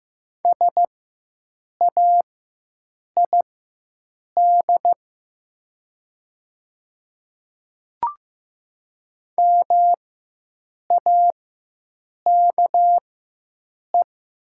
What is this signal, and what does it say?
Morse code, heard as a keyed tone, about 15 wpm, spaced as for 7 wpm, 700 Hz. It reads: SAID MAKE
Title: 4x Character Spacing - Top 100 Words - 15wpm